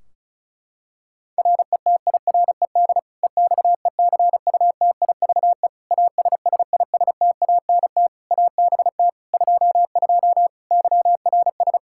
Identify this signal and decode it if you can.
Morse code, heard as a keyed tone, about 34 wpm, 700 Hz. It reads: RETIRED EXECUTIVE ASSISTANT ABT 22 YRS
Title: Practice QSO - Exchange 3 - More Info - 35wpm